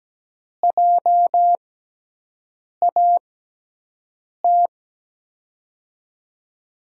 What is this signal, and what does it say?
Morse code, heard as a keyed tone, about 17 wpm, spaced as for 6 wpm, 700 Hz. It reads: JAT
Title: USA Call Signs - 6 Characters - 6x Character Spacing - 17wpm